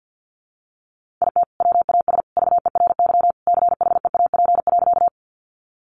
Morse code, heard as a keyed tone, about 50 wpm, 700 Hz. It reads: IT WAS VERY CHEAP.
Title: Sentences from Top 1200 Words 50wpm